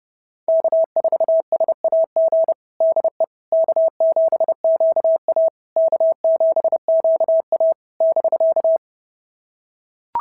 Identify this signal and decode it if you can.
Morse code, heard as a keyed tone, about 28 wpm, 690 Hz. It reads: K4SAG DE K7QA K7QA <BK>
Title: SOTA QSO Practice 30wpm